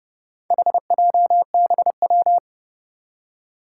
Morse code, heard as a keyed tone, about 30 wpm, 700 Hz. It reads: HJBW
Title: Four Challenging Letter-Number Combinations 30wpm